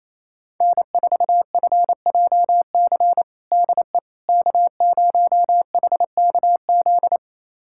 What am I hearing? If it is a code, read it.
Morse code, heard as a keyed tone, about 27 wpm, 700 Hz. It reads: N4FJC DE K0HKZ
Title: USA Callsign DE Callsign - 5 Characters - 28wpm